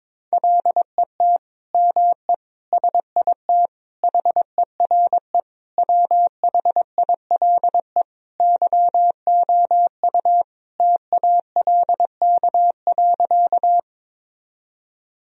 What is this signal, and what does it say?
Morse code, heard as a keyed tone, about 22 wpm, 700 Hz. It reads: LET ME SIT HERE WHILE YOU TALK.
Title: Sentences from Top 1000 Words - Review Part 1 - 22wpm